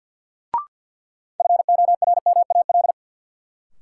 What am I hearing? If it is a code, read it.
Morse code, heard as a keyed tone, about 50 wpm, 700 Hz. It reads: FORMAL
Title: Sentences from Top 1300 Words - New Word Drill Alt - 50wpm